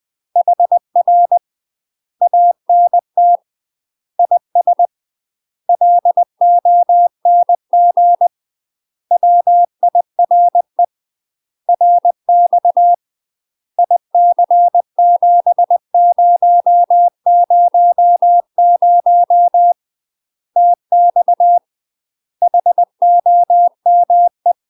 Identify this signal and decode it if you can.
Morse code, heard as a keyed tone, about 20 wpm, 700 Hz. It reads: HR ANT IS LONG WIRE RX IC7000 TX HOME
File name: UK MASTER QSOs TRAINING / 20WPM / UK MASTER QSOs 0015